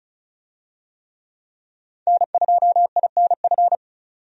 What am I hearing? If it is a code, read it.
Morse code, heard as a keyed tone, about 36 wpm, 700 Hz. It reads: N2INF
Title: USA Call Signs - 5 Characters - 35wpm